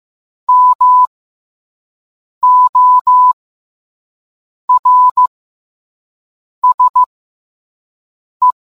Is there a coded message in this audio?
Yes: Morse code from 0.5 to 8.5 s, heard as a keyed tone, about 15 wpm, spaced as for 5 wpm, 1000 Hz. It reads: MORSE